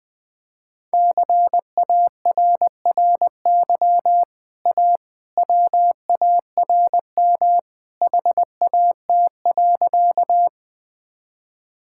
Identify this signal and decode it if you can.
Morse code, heard as a keyed tone, about 20 wpm, 700 Hz. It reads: CARRY A WARM HAT.
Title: Easy Sentences with Hints - 20wpm